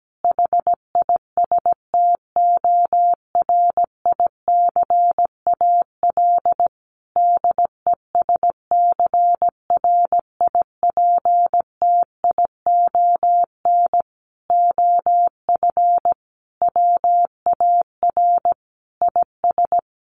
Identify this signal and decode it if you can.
Morse code, heard as a keyed tone, about 17 wpm, 700 Hz. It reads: HISTORICAL DESCRIPTION OF WAR IS
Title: Sentences from Top 1300 Words 17wpm